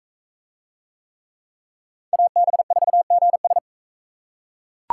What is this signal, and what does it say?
Morse code, heard as a keyed tone, about 40 wpm, 700 Hz. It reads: AB4GS